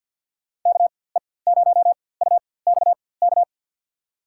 Morse code, heard as a keyed tone, about 53 wpm, spaced as for 24 wpm, 700 Hz. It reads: KE0UXK
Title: USA Call Signs - 6 Characters - 4x Character Spacing - 50wpm